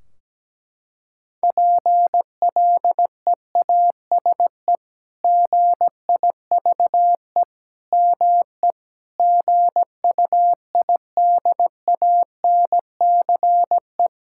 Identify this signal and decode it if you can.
Morse code, heard as a keyed tone, about 17 wpm, 700 Hz. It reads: PLEASE GIVE ME GUIDANCE
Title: Sentences from Top 1600 Words 17wpm